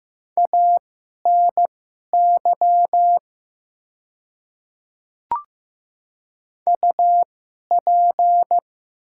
Morse code, heard as a keyed tone, about 15 wpm, spaced as for 11 wpm, 700 Hz. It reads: ANY UP
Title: 2x Character Spacing - Top 300 Words - 15wpm